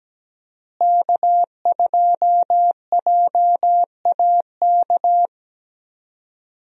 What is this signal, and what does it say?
Morse code, heard as a keyed tone, about 17 wpm, 700 Hz. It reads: K2JAK
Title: USA Call Signs - 5 Characters - Repeated 2x - 17wpm